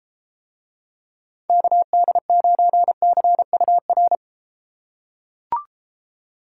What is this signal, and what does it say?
Morse code, heard as a keyed tone, about 32 wpm, 700 Hz. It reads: KD9CUR